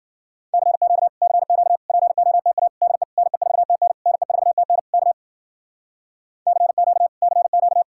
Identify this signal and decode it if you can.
Morse code, heard as a keyed tone, about 49 wpm, 700 Hz. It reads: CQ CQ POTA DE NE4TN NE4TN K CQ CQ
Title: Speed-Racing Common QSO Phrases Encore - 40wpm